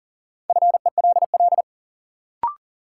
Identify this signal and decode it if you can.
Morse code, heard as a keyed tone, about 39 wpm, 700 Hz. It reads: FELL
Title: Sentences from Top 1600 Words - New Word Drill - 40wpm